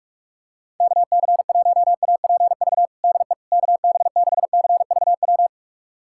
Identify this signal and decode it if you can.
Morse code, heard as a keyed tone, about 44 wpm, 700 Hz. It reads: KC1APV DE KB6CVW
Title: USA Callsign DE Callsign 45wpm